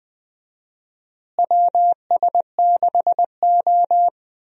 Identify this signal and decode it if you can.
Morse code, heard as a keyed tone, about 20 wpm, 700 Hz. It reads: WS6O
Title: USA Call Signs - 4 Characters - Repeated 3x - 20wpm